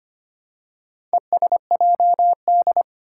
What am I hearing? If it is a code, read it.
Morse code, heard as a keyed tone, about 25 wpm, 700 Hz. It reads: ESJD
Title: Four Challenging Letter-Number Combinations 25wpm